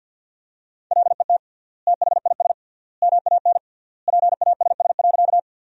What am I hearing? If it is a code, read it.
Morse code, heard as a keyed tone, about 50 wpm, 700 Hz. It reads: LET THIS MAN PASS.
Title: 3x Word-Spacing - Sentences from Top 400 Words - 50wpm